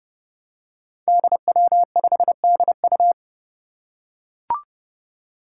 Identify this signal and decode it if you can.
Morse code, heard as a keyed tone, about 30 wpm, 700 Hz. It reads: DW5DU
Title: Five Challenging Letter-Number Combinations 30wpm